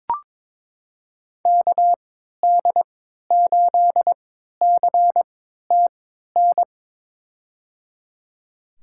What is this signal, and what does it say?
Morse code, heard as a keyed tone, about 22 wpm, spaced as for 12 wpm, 700 Hz. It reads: KD8CTN
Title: USA Call Signs - 3x Character Spacing - 22wpm